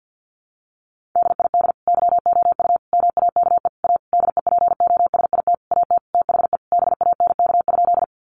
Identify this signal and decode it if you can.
Morse code, heard as a keyed tone, about 50 wpm, 700 Hz. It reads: DID YOU MAKE A DEPOSIT AT THE BANK?
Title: Sentences from Top 1300 Words 50wpm